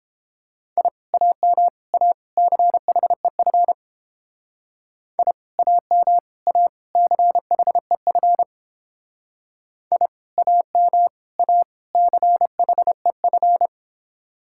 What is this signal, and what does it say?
Morse code, heard as a keyed tone, about 29 wpm, 700 Hz. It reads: I AM A CHEF I AM A CHEF I AM A CHEF